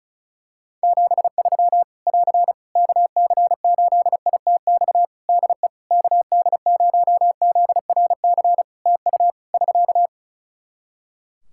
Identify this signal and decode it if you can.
Morse code, heard as a keyed tone, about 36 wpm, 700 Hz. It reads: 73 <AR> KC8ITX DE KD0ZRC TU <SK>
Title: Practice QSO - Exchange 4 - Ending - 35wpm